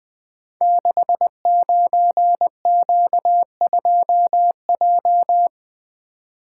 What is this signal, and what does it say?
Morse code, heard as a keyed tone, about 20 wpm, 700 Hz. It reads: 69Q2J
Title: Code Groups - 5 Letter:Numbers - 20wpm